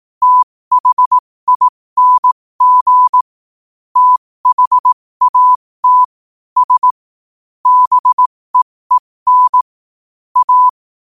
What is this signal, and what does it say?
Morse code, heard as a keyed tone, about 18 wpm, spaced as for 15 wpm, 1000 Hz. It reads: THING THAT S BEEN A
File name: qotdfiles / 15wpm